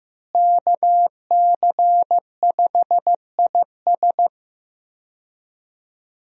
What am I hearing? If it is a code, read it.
Morse code, heard as a keyed tone, about 15 wpm, 700 Hz. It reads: KC5IS
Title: USA Call Signs - 5 Characters - Repeated 3x - 15wpm